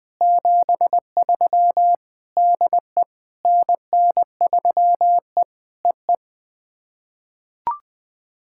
Morse code, heard as a keyed tone, about 20 wpm, 700 Hz. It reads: 73 DE NN3E EE
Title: POTA QSO Practice - 20wpm